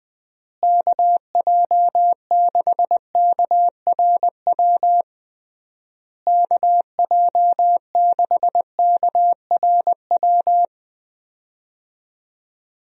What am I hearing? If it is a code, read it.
Morse code, heard as a keyed tone, about 20 wpm, 700 Hz. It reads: KJ6KRW KJ6KRW
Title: USA Call Signs - 6 Characters - Repeated 3x - 20wpm